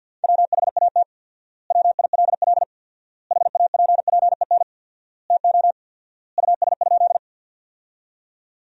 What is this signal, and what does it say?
Morse code, heard as a keyed tone, about 50 wpm, 700 Hz. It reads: WHAT WILL HAPPEN TO US?
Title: 4x Word-Spacing - Sentences from Top 400 Words - 50wpm